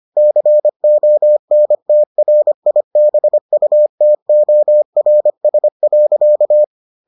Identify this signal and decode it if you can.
Morse code, heard as a keyed tone, about 25 wpm, 600 Hz. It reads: CONTRIBUTORS.